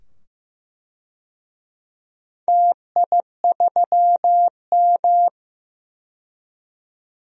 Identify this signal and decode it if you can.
Morse code, heard as a keyed tone, about 15 wpm, 700 Hz. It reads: TI3M
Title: DXCC-100 Call Signs - 4 Characters - 15wpm